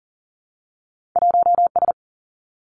Morse code, heard as a keyed tone, about 43 wpm, 700 Hz. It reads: 1S
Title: Two Challenging Letter-Number Combinations 40wpm